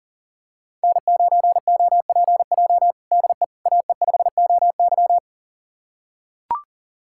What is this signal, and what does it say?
Morse code, heard as a keyed tone, about 40 wpm, 700 Hz. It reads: N9OPJ DE AE5OY
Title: USA Callsign DE Callsign - 5 Characters - 40wpm